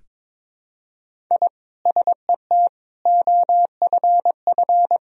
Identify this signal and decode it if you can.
Morse code, heard as a keyed tone, about 22 wpm, 700 Hz. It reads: I SET OFF
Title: Sentences from Top 1000 Words - Review Part 1 - 22wpm